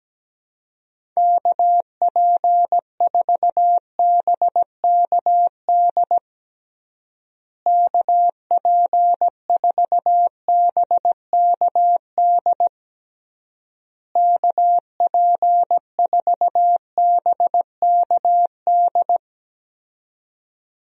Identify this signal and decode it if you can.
Morse code, heard as a keyed tone, about 17 wpm, 700 Hz. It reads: KP4BKD KP4BKD KP4BKD
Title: DXCC-100 Call Signs - 6 Characters - Repeated 3x - 17wpm